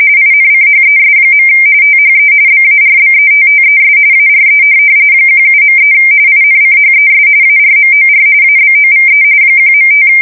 RTTY, about 45.45 baud. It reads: INCHES. THE BARBETTES ARE 17.3 INCHES ARMOR ABOVE 1.5
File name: fsk170 / USS-New-Jersey